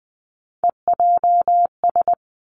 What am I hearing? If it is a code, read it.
Morse code, heard as a keyed tone, about 20 wpm, 700 Hz. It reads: EJS
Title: Three Challenging Letter-Number Combinations 20wpm